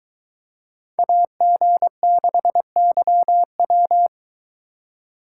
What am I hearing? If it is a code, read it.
Morse code, heard as a keyed tone, about 22 wpm, 700 Hz. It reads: AG6YW